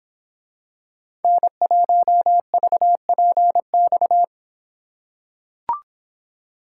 Morse code, heard as a keyed tone, about 25 wpm, 700 Hz. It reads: N1VPX